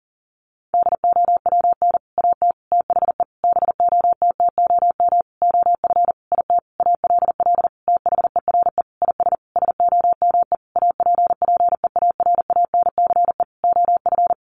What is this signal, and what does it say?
Morse code, heard as a keyed tone, about 40 wpm, 700 Hz. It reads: DOWN AT THE BOTTOM OF IT ALL THERE IS SOME APPEARANCE OF
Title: Sentences from Top 1100 Words 40wpm